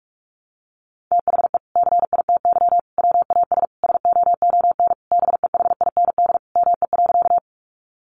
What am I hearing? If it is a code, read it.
Morse code, heard as a keyed tone, about 44 wpm, 700 Hz. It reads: THE CITY WAS SOON BEHIND ME.
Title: Sentences from Top 500 Words 45wpm